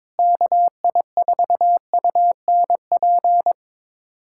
Morse code, heard as a keyed tone, about 22 wpm, 700 Hz. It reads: KI4UNP
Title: USA Callsign DE Callsign - 6 Characters - 22wpm